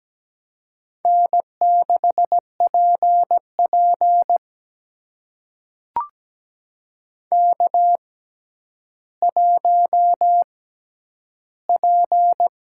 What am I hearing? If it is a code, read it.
Morse code, heard as a keyed tone, about 17 wpm, spaced as for 10 wpm, 700 Hz. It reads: N6PP K1P
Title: USA Call Signs - 4 Characters - 6x Character Spacing - 17wpm